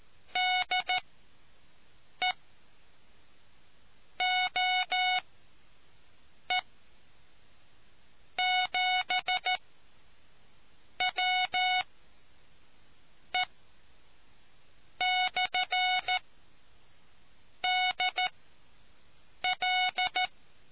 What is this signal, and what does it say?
Morse code, heard as a keyed tone, about 14 wpm, spaced as for 5 wpm, 740 Hz. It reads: DE OE7WE/DL